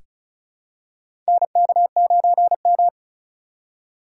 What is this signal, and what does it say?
Morse code, heard as a keyed tone, about 34 wpm, 700 Hz. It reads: NK9M
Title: USA Call Signs - 4 Characters - 4x Character Spacing - 35wpm